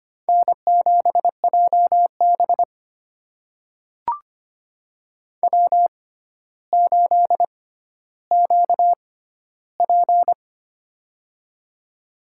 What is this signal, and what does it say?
Morse code, heard as a keyed tone, about 25 wpm, spaced as for 13 wpm, 700 Hz. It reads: N7JB W8QP
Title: USA Call Signs - 4 Characters - 6x Character Spacing - 25wpm